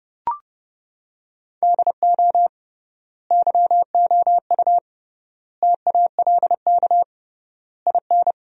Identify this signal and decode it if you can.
Morse code, heard as a keyed tone, about 30 wpm, 700 Hz. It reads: DO YOU TALK IN
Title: 3x Word-Spacing - Sentences from Top 500 Words - 30wpm